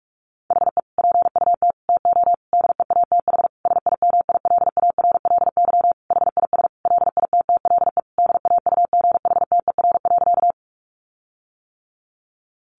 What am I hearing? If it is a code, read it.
Morse code, heard as a keyed tone, about 44 wpm, 700 Hz. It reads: HE PUT TO DEATH SIMILARLY HIS LITTLE DAUGHTER.
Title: Sentences from Top 1500 Words 45wpm